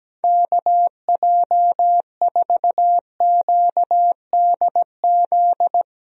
Morse code, heard as a keyed tone, about 17 wpm, 700 Hz. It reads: KJ4QDZ
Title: USA Call Signs - 6 Characters - 17wpm